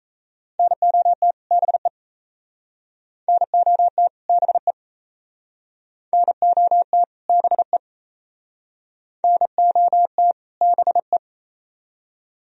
Morse code, heard as a keyed tone, about 36 wpm, 700 Hz. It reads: NOT BE NOT BE NOT BE NOT BE